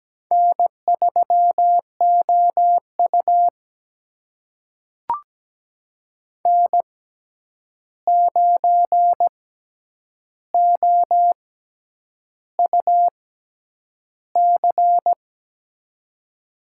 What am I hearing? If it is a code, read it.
Morse code, heard as a keyed tone, about 17 wpm, spaced as for 8 wpm, 700 Hz. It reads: N3OU N9OUC
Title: USA Call Signs - 6x Character Spacing - 17wpm